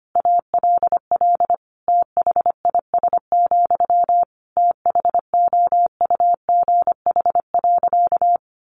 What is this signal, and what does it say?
Morse code, heard as a keyed tone, about 25 wpm, 700 Hz. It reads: ALL THIS, THOUGH.
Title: Sentences from Top 1400 Words 25wpm